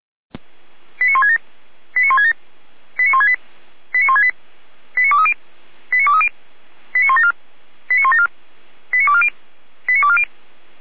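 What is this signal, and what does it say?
Five-tone selective call (ZVEI), heard as quick runs of five tones, tones of about 70 ms. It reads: 89167 89167 89168 89168 89230 89230 89174 89174 89240 89240
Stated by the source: ZVEI five-tone call received on SDR.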